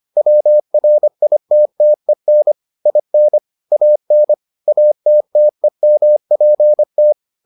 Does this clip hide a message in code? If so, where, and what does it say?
Yes: Morse code from 0.2 to 7.1 s, heard as a keyed tone, about 25 wpm, 600 Hz. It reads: WRITTEN IN AN ATTEMPT